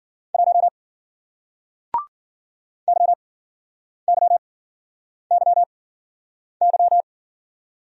Morse code, heard as a keyed tone, about 55 wpm, spaced as for 9 wpm, 700 Hz. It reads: 1 YYYY